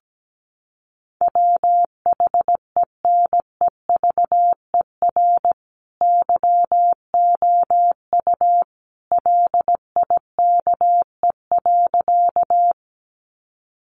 Morse code, heard as a keyed tone, about 17 wpm, 700 Hz. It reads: WHENEVER YOU LIKE.
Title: Sentences from Top 1200 Words 17wpm